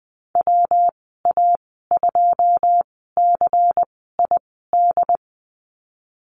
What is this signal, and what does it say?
Morse code, heard as a keyed tone, about 20 wpm, spaced as for 15 wpm, 700 Hz. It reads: WA2CID
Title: 147 Lesson 24g - Letter y - Call Signs 20wpm